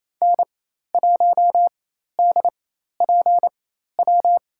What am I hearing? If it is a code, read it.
Morse code, heard as a keyed tone, about 28 wpm, spaced as for 13 wpm, 700 Hz. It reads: N1DPW
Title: USA Call Signs - 5 Characters - 4x Character Spacing - 28wpm